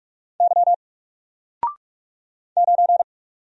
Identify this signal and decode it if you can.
Morse code, heard as a keyed tone, about 44 wpm, 700 Hz. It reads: Y 9
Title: Single Letter-Number 45wpm